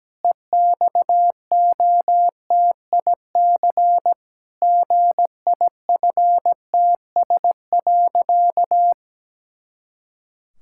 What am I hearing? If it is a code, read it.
Morse code, heard as a keyed tone, about 17 wpm, 700 Hz. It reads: EXOTIC GIFTS.
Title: Pangram Challenge v2 - 17wpm